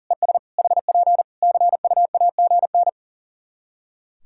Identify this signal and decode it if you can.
Morse code, heard as a keyed tone, about 40 wpm, 700 Hz. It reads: ES HP CUAGN
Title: Practice QSO - Exchange 4 - Ending - 40wpm